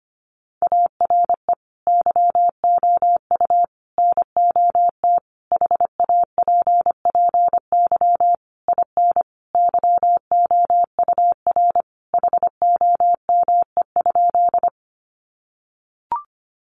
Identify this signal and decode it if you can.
Morse code, heard as a keyed tone, about 25 wpm, 700 Hz. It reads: ARE YOU NOT HAPPY IN YOUR HOME?
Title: Sentences from Top 1000 Words 25wpm